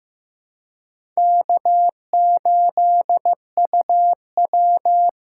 Morse code, heard as a keyed tone, about 15 wpm, 700 Hz. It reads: K8UW
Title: USA Call Signs - 4 Characters - 2x Character Spacing - 15wpm